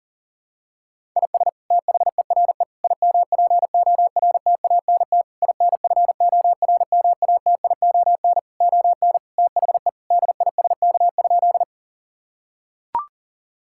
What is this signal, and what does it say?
Morse code, heard as a keyed tone, about 40 wpm, 700 Hz. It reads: IS THERE IMPORTANT INFORMATION ON THE DISK?
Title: Sentences from Top 1000 Words - Review Part 6 - 40wpm